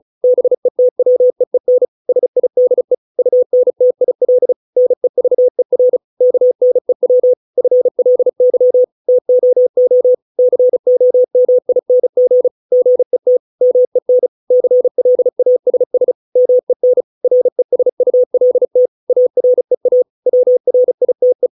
Morse code, heard as a keyed tone, about 34 wpm, 500 Hz. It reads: BETWEEN SIDE UNTIL NEVER KNEW FLY TOO COMING GET MEN CLASS MEN RESULT AREA WRITE